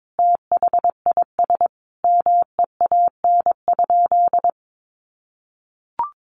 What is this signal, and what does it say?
Morse code, heard as a keyed tone, about 22 wpm, 700 Hz. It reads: THIS MEAN?
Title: Sentences from Top 200 Words 22wpm